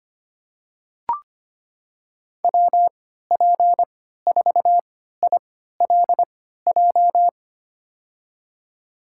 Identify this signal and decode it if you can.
Morse code, heard as a keyed tone, about 25 wpm, spaced as for 14 wpm, 700 Hz. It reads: WP4ILJ
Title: USA Call Signs - 6 Characters - 3x Character Spacing - 25wpm